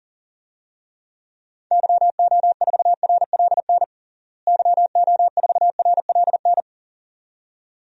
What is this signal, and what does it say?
Morse code, heard as a keyed tone, about 40 wpm, 700 Hz. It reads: YO4RLN YO4RLN
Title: DXCC-100 Call Signs - 6 Characters - Repeated 2x - 40wpm